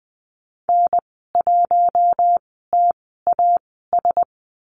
Morse code, heard as a keyed tone, about 20 wpm, spaced as for 15 wpm, 700 Hz. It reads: N1TAS
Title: 069 Lesson 12g - Letter d - Call Signs 20wpm